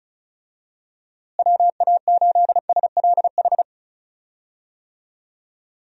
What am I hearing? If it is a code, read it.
Morse code, heard as a keyed tone, about 34 wpm, 700 Hz. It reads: WA8SLH
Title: USA Call Signs - 6 Characters - Repeated 3x - 35wpm